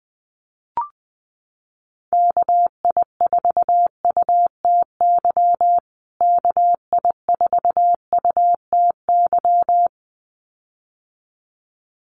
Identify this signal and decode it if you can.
Morse code, heard as a keyed tone, about 20 wpm, 700 Hz. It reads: KI4UTY KI4UTY
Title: USA Call Signs Repeated 2x 20wpm